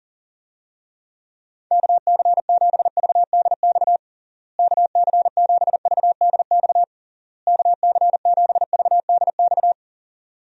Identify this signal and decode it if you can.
Morse code, heard as a keyed tone, about 37 wpm, 700 Hz. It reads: KC7VDX KC7VDX KC7VDX
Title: USA Call Signs - 6 Characters - Repeated 3x - 40wpm